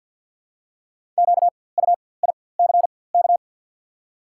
Morse code, heard as a keyed tone, about 52 wpm, spaced as for 24 wpm, 690 Hz. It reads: QUICK
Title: 4x Character Spacing - Top 500 Words - 50wpm